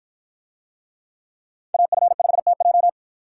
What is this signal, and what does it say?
Morse code, heard as a keyed tone, about 53 wpm, 700 Hz. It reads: AF5TJ